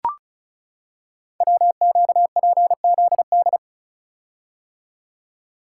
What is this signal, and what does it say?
Morse code, heard as a keyed tone, about 36 wpm, 700 Hz. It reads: WQPZD
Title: Code Groups - 5 Letter:Numbers - 35wpm